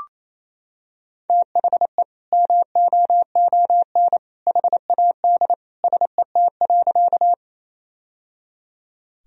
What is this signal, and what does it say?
Morse code, heard as a keyed tone, about 28 wpm, 700 Hz. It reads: THE MOON HAD SET.
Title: Sentences from Top 1000 Words - Review Part 3 - 28wpm